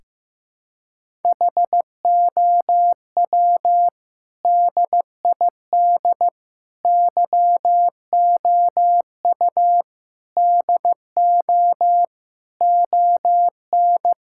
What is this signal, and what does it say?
Morse code, heard as a keyed tone, about 15 wpm, 700 Hz. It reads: HOW DID YOU DO ON
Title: Sentences from Top 1000 Words - Review Part 5 - 15wpm